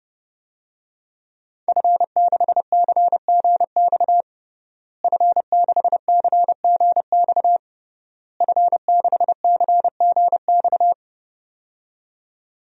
Morse code, heard as a keyed tone, about 30 wpm, 700 Hz. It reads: F6CGX F6CGX F6CGX
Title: DXCC-100 Call Signs - 5 Characters - Repeated 3x - 30wpm